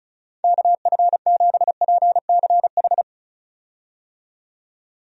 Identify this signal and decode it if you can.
Morse code, heard as a keyed tone, about 36 wpm, 700 Hz. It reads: KF7PCH
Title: USA Call Signs - 6 Characters - 35wpm